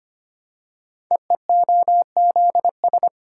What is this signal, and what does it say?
Morse code, heard as a keyed tone, about 25 wpm, 700 Hz. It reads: EEOZS
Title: Code Groups - 5 Letters - 25wpm